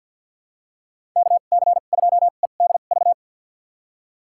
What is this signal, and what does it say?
Morse code, heard as a keyed tone, about 52 wpm, spaced as for 37 wpm, 700 Hz. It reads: KC2EDV
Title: USA Call Signs - 2x Character Spacing - 50wpm